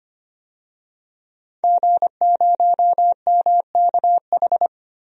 Morse code, heard as a keyed tone, about 25 wpm, 700 Hz. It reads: G0MKH
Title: DXCC-100 Call Signs - 5 Characters - Repeated 2x - 25wpm